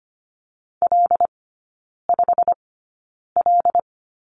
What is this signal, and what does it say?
Morse code, heard as a keyed tone, about 25 wpm, spaced as for 9 wpm, 700 Hz. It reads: L5L
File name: Lesson 15e - Number 5 - Three Characters 25wpm